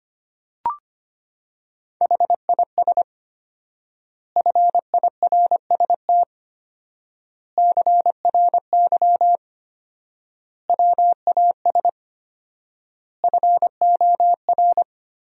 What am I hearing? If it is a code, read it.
Morse code, heard as a keyed tone, about 25 wpm, 700 Hz. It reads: HIS FIRST CRY WAS FOR
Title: 4x Word-Spacing - Sentences from Top 500 Words - 25wpm